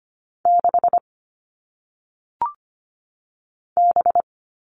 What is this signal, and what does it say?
Morse code, heard as a keyed tone, about 25 wpm, 700 Hz. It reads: 6 B
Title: All Cut Numbers - 1-Digit 25wpm